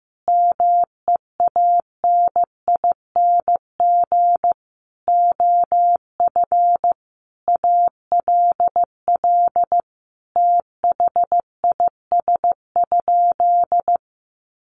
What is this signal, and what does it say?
Morse code, heard as a keyed tone, about 15 wpm, 700 Hz. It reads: MEANING OF ALL THIS?
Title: Sentences from Top 900 Words 15wpm